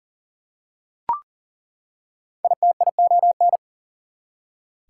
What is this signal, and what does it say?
Morse code, heard as a keyed tone, about 40 wpm, 700 Hz. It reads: ITION
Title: Prefixes and Suffixes 40wpm